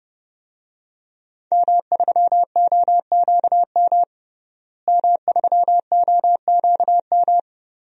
Morse code, heard as a keyed tone, about 30 wpm, 700 Hz. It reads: M3OQM M3OQM
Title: DXCC-100 Call Signs - 5 Characters - Repeated 3x - 30wpm